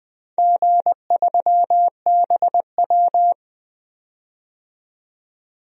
Morse code, heard as a keyed tone, about 20 wpm, 700 Hz. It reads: G3BW
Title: DXCC-100 Call Signs - 4 Characters - 20wpm